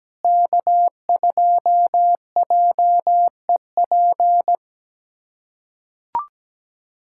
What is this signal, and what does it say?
Morse code, heard as a keyed tone, about 17 wpm, 700 Hz. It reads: K2JEP
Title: USA Callsign DE Callsign - 5 Characters - 17wpm